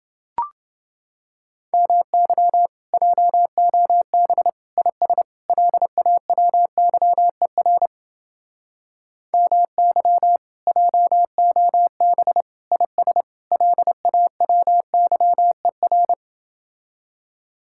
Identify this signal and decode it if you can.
Morse code, heard as a keyed tone, about 29 wpm, 700 Hz. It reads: MY JOB IS LAWYER MY JOB IS LAWYER